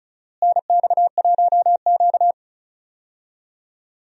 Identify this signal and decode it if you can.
Morse code, heard as a keyed tone, about 36 wpm, 700 Hz. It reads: NX1Q
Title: USA Call Signs - 4 Characters - Repeated 3x - 35wpm